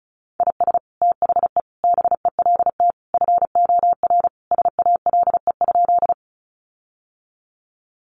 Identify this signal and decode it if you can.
Morse code, heard as a keyed tone, about 34 wpm, 700 Hz. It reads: IS THE BELT FOR SALE?
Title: Sentences from Top 1400 Words 35wpm